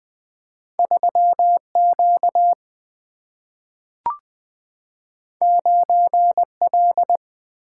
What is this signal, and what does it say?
Morse code, heard as a keyed tone, about 21 wpm, 700 Hz. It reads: 3Q 9L
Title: Single-Letter Single-Number 20wpm